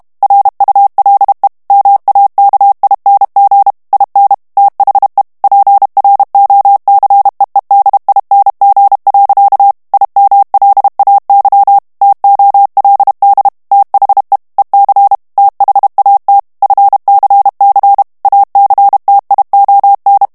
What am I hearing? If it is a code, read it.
Morse code, heard as a keyed tone, about 32 wpm, 800 Hz. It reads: RULE MAKING IN THE PROCEEDING. IMLAY TOLD THE EC THAT FCC ACTION
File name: ANGLAIS texte clair 3 vitesse 30 mots minute